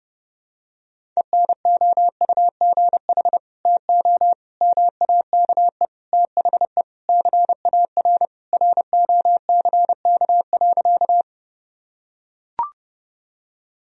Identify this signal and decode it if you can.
Morse code, heard as a keyed tone, about 32 wpm, 700 Hz. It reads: ENOUGH TO MAKE THE CAR ROCK.
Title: Sentences from Top 1000 Words - Review Part 1 - 30wpm